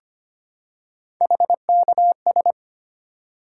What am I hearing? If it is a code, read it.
Morse code, heard as a keyed tone, about 25 wpm, 700 Hz. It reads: HKS
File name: Three Letter-Number Combinations - 25wpm